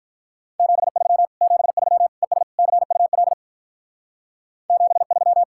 Morse code, heard as a keyed tone, about 44 wpm, 700 Hz. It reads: 73 73 ES CUL 73